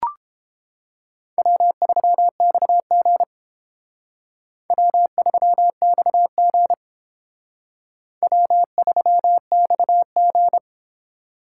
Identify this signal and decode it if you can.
Morse code, heard as a keyed tone, about 32 wpm, 700 Hz. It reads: W3XG W3XG W3XG